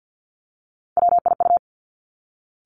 Morse code, heard as a keyed tone, about 50 wpm, 700 Hz. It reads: WIU
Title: Three Challenging Letter-Number Combinations 50wpm